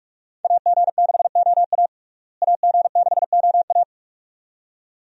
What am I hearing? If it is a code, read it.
Morse code, heard as a keyed tone, about 44 wpm, 700 Hz. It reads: AG6OA AG6OA
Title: USA Call Signs - 5 Characters - Repeated 2x - 45wpm